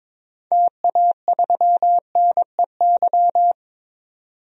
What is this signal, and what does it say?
Morse code, heard as a keyed tone, about 22 wpm, 700 Hz. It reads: TA3NEY
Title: DXCC-100 Call Signs - 6 Characters - 22wpm